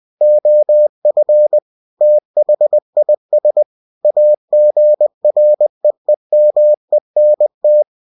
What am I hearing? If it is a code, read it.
Morse code, heard as a keyed tone, about 20 wpm, 600 Hz. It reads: OF THIS AGREEMENT
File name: cw-books / winnie-the-pooh / winnie-20wpm-page-0113